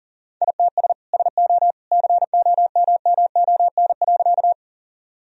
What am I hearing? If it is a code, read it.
Morse code, heard as a keyed tone, about 40 wpm, 700 Hz. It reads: ITS SO COMMON.
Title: Sentences from Top 1000 Words - Review Part 3 - 40wpm